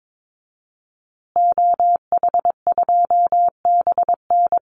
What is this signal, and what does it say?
Morse code, heard as a keyed tone, about 22 wpm, 700 Hz. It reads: OH2BN
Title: Call Signs Repeated 2x - 22wpm